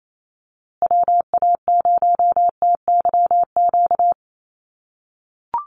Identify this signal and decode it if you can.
Morse code, heard as a keyed tone, about 28 wpm, 700 Hz. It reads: WA0TYQ
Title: USA Call Signs Repeated 2x 28wpm